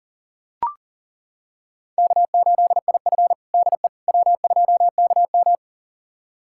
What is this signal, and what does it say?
Morse code, heard as a keyed tone, about 40 wpm, 700 Hz. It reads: K8IF DE W2KM
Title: USA Callsign DE Callsign - 4 Characters - 40wpm